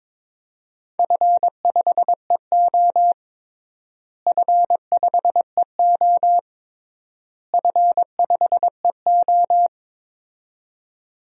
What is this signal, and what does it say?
Morse code, heard as a keyed tone, about 22 wpm, 700 Hz. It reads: F5EO F5EO F5EO
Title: DXCC-100 Call Signs - 4 Characters - Repeated 3x - 22wpm